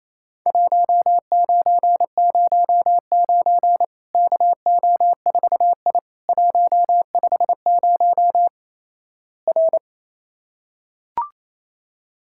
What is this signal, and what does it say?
Morse code, heard as a keyed tone, about 28 wpm, 700 Hz. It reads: 1909 KO4I 150 R
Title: Worked All Europe DX Contest - QTC Exchange - 28wpm